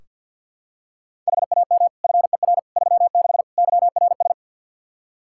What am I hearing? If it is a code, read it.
Morse code, heard as a keyed tone, about 48 wpm, 700 Hz. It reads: HAM FER 36 YRS
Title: Practice QSO - Exchange 3 - More Info - 50wpm